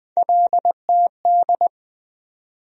Morse code, heard as a keyed tone, about 20 wpm, 700 Hz. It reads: LTD
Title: USA Call Sign Postfixes - 3 Letters 20wpm